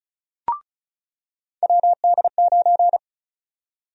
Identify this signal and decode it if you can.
Morse code, heard as a keyed tone, about 36 wpm, 700 Hz. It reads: WD9
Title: USA Call Sign Prefixes - 35wpm